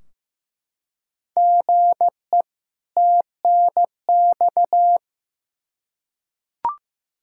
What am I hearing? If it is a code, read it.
Morse code, heard as a keyed tone, about 15 wpm, 700 Hz. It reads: GE TNX
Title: Practice QSO - Exchange 1 - Beginning - 15wpm